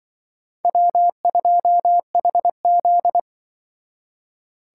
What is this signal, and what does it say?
Morse code, heard as a keyed tone, about 24 wpm, 700 Hz. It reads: W2HZ
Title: Speed Racing - USA 1x2 Call Signs 20wpm